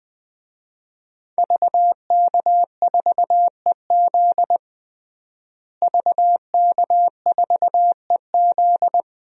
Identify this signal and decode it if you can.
Morse code, heard as a keyed tone, about 21 wpm, 700 Hz. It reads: VK4EZ VK4EZ
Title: DXCC-100 Call Signs - 5 Characters - Repeated 3x - 20wpm